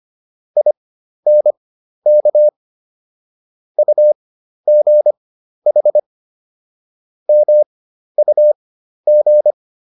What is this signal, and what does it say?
Morse code, heard as a keyed tone, about 25 wpm, spaced as for 12 wpm, 600 Hz. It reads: INK UGH MUG